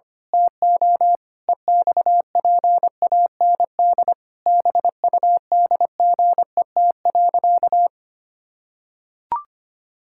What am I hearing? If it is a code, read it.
Morse code, heard as a keyed tone, about 25 wpm, 700 Hz. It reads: TO EXPAND BUDGET.
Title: Pangram Challenge v2 - 25wpm